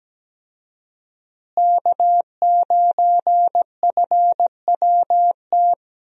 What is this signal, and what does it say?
Morse code, heard as a keyed tone, about 17 wpm, 700 Hz. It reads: K9FWT
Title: USA Call Signs - 5 Characters - Repeated 3x - 17wpm